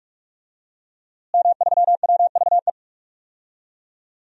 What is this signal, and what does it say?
Morse code, heard as a keyed tone, about 46 wpm, 700 Hz. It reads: M3WVE
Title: DXCC-100 Call Signs - 5 Characters - Repeated 3x - 45wpm